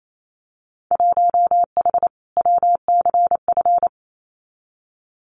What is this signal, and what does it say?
Morse code, heard as a keyed tone, about 28 wpm, 700 Hz. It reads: 1H WCF
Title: Winter Field Day Exchanges 28wpm